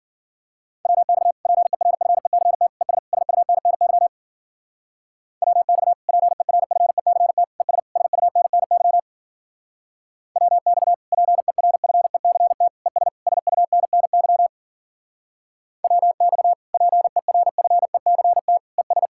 Morse code, heard as a keyed tone, about 52 wpm, 700 Hz. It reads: WX PERFECT ES SUNNY WX PERFECT ES SUNNY WX PERFECT ES SUNNY WX PERFECT ES